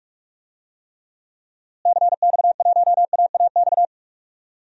Morse code, heard as a keyed tone, about 43 wpm, 700 Hz. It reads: CX1AAX
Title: DXCC-100 Call Signs - 6 Characters - Repeated 3x - 45wpm